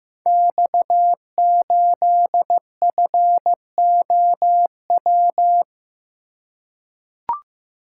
Morse code, heard as a keyed tone, about 15 wpm, 700 Hz. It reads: X8FOW
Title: Code Groups - 5 Letter:Numbers - 15wpm